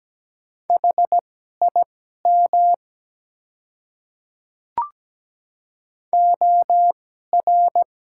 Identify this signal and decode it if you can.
Morse code, heard as a keyed tone, about 17 wpm, spaced as for 12 wpm, 700 Hz. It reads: HIM OR
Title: 2x Character Spacing - Top 100 Words - 17wpm